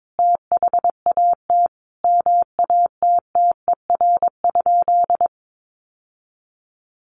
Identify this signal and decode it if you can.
Morse code, heard as a keyed tone, about 22 wpm, 700 Hz. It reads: THAT MATTER?
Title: Sentences from Top 700 Words 22wpm